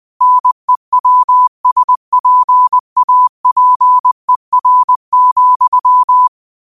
Morse code, heard as a keyed tone, about 20 wpm, 1000 Hz. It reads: NEWSPAPER,